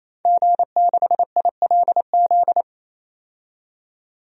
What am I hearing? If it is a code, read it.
Morse code, heard as a keyed tone, about 28 wpm, 700 Hz. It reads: G6ILZ
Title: DXCC-100 Call Signs - 5 Characters - 28wpm